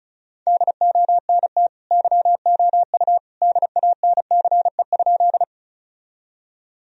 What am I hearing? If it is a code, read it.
Morse code, heard as a keyed tone, about 36 wpm, 700 Hz. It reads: DONT YOU DANCE?
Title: Sentences from Top 1000 Words - Review Part 4 - 35wpm